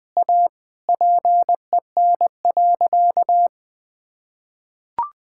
Morse code, heard as a keyed tone, about 21 wpm, 700 Hz. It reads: A PEN.
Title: Sentences with 2-Word Phrases - Spoken Beginning - 20wpm